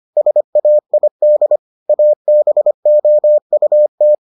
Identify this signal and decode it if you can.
Morse code, heard as a keyed tone, about 25 wpm, 600 Hz. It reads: SAID ABOUT